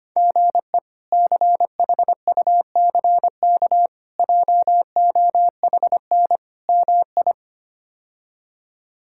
Morse code, heard as a keyed tone, about 25 wpm, 700 Hz. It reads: GE CHUCK JOHN MI
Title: K1USN SST QSO Practice - 25wpm